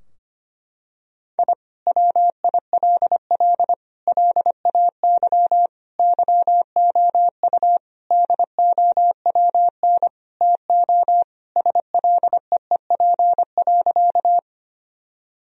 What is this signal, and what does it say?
Morse code, heard as a keyed tone, about 25 wpm, 700 Hz. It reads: I WILL LAY YOU DOWN TO SLEEP.
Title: Sentences from Top 1000 Words - Review Part 2 - 25wpm